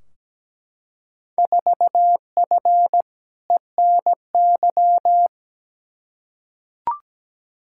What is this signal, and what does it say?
Morse code, heard as a keyed tone, about 17 wpm, 700 Hz. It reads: 4F ENY
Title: Field Day Exchanges - 17wpm